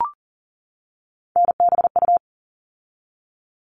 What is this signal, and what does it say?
Morse code, heard as a keyed tone, about 40 wpm, 700 Hz. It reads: NBU
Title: All Cut Numbers - 3-Digit 40wpm